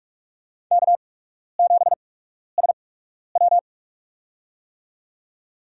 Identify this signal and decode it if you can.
Morse code, heard as a keyed tone, about 46 wpm, spaced as for 12 wpm, 700 Hz. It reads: K7SW
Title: USA Call Signs - 4 Characters - 8x Character Spacing - 45wpm